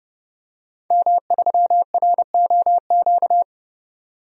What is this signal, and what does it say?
Morse code, heard as a keyed tone, about 32 wpm, 700 Hz. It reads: M3ROQ
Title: DXCC-100 Call Signs - 5 Characters - 30wpm